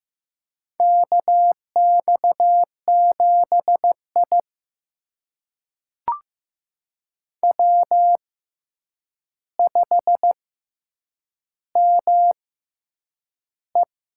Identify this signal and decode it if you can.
Morse code, heard as a keyed tone, about 15 wpm, spaced as for 8 wpm, 700 Hz. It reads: KX7I W5ME
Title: USA Call Signs - 4 Characters - 6x Character Spacing - 15wpm